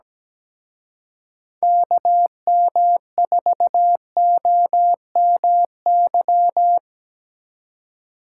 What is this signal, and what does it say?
Morse code, heard as a keyed tone, about 20 wpm, 700 Hz. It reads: KM4OMY
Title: Speed Racing - USA 2x3 Call Signs 17wpm